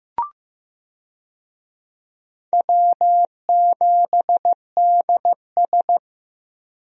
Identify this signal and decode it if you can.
Morse code, heard as a keyed tone, about 14 wpm, 700 Hz. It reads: W7DS
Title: USA Call Signs - 4 Characters - Repeated 3x - 15wpm